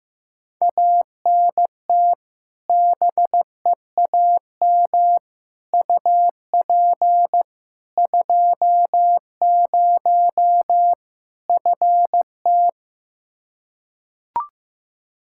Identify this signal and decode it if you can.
Morse code, heard as a keyed tone, about 15 wpm, 700 Hz. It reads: ANT BEAM UP 20 FT
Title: K1USN SST QSO Practice 15wpm